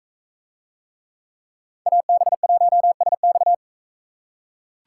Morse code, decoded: AB1SX

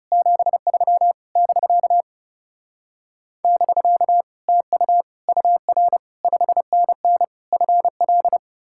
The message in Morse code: 73 <BK> <BK> TU UR 5NN FL

